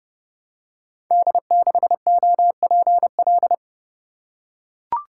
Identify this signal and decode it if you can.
Morse code, heard as a keyed tone, about 30 wpm, 700 Hz. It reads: D6OPL
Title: Code Groups - 5 Letter:Numbers - 30wpm